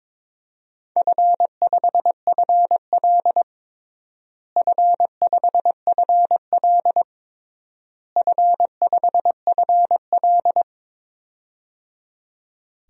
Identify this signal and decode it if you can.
Morse code, heard as a keyed tone, about 21 wpm, 700 Hz. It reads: F5FL F5FL F5FL
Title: DXCC-100 Call Signs - 4 Characters - Repeated 3x - 22wpm